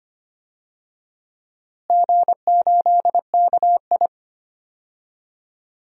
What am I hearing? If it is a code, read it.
Morse code, heard as a keyed tone, about 25 wpm, 700 Hz. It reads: G8KI
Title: DXCC-100 Call Signs - 4 Characters - Repeated 2x - 25wpm